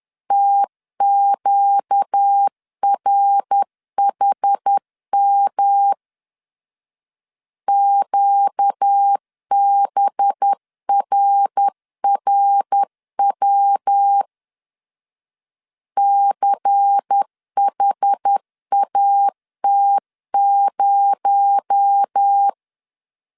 Morse code, heard as a keyed tone, about 10 wpm, 800 Hz. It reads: TQRHM QBRRW CHAT0
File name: CW / DET / 1 6 1500C S B 10wpm 4137